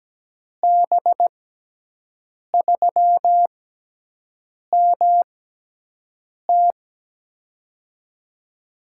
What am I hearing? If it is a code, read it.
Morse code, heard as a keyed tone, about 17 wpm, 700 Hz. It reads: B 3 M T
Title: USA Call Signs - 5 Characters - 6x Character Spacing - 17wpm